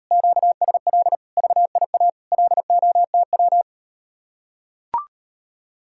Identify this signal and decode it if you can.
Morse code, heard as a keyed tone, about 38 wpm, 700 Hz. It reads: QSL VIA LOTW